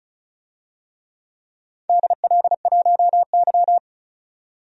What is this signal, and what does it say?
Morse code, heard as a keyed tone, about 36 wpm, 700 Hz. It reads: DL1Y
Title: DXCC-100 Call Signs - 4 Characters - 35wpm